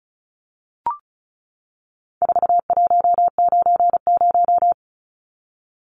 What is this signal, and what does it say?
Morse code, heard as a keyed tone, about 34 wpm, 700 Hz. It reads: 4190
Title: Numbers - 4-Digit 35wpm